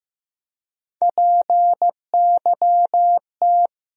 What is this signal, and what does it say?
Morse code, heard as a keyed tone, about 15 wpm, 700 Hz. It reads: PYT
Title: USA Call Sign Postfixes - 3 Letters 15wpm